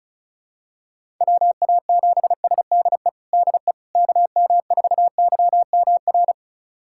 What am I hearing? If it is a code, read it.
Morse code, heard as a keyed tone, about 34 wpm, 700 Hz. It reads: WA7SDE DE KM4YMR
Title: USA Callsign DE Callsign - 6 Characters - 35wpm